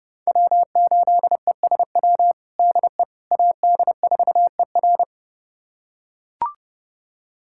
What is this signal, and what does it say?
Morse code, heard as a keyed tone, about 32 wpm, 700 Hz. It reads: W8ESW DE AD4ER
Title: USA Callsign DE Callsign - 5 Characters - 30wpm